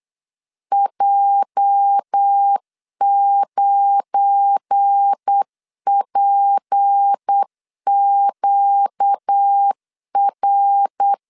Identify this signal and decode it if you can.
Morse code, heard as a keyed tone, about 8 wpm, 800 Hz. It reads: J9PQR
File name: LEZIONI / 9 28 8wpm 3512